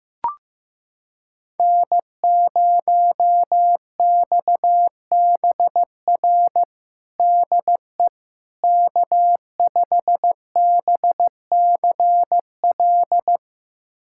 Morse code, heard as a keyed tone, about 16 wpm, 700 Hz. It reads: N0XBR DE K5BCL
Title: USA Callsign DE Callsign - 5 Characters - 15wpm